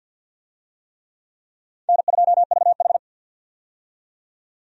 Morse code, heard as a keyed tone, about 53 wpm, 700 Hz. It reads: N2VH
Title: USA Call Signs - 4 Characters - 50wpm